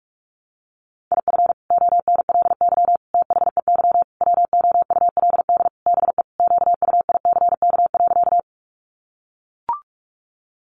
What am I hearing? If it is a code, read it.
Morse code, heard as a keyed tone, about 44 wpm, 700 Hz. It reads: IF ONLY THEY WOULD BE QUICK.
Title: Sentences from Top 500 Words 45wpm